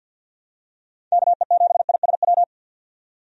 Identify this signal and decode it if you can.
Morse code, heard as a keyed tone, about 52 wpm, 700 Hz. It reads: KE7ISW
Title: USA Call Signs - 6 Characters - 50wpm